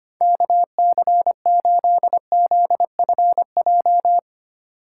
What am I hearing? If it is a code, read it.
Morse code, heard as a keyed tone, about 25 wpm, 700 Hz. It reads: KC8ZFJ
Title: USA Call Signs - 6 Characters - 3x Character Spacing - 25wpm